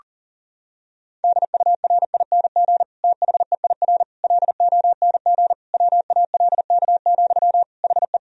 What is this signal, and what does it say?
Morse code, heard as a keyed tone, about 40 wpm, 700 Hz. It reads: DURING THEIR LONG WALK, HE
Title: Sentences from Top 1000 Words - Review Part 2 - 40wpm